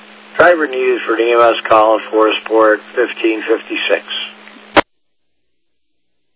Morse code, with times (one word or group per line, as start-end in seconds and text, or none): none